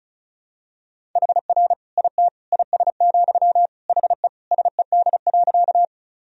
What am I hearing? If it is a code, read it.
Morse code, heard as a keyed tone, about 34 wpm, 700 Hz. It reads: HR IT IS, HE SED.
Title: Sentences from Top 200 Words and Abbreviations 35wpm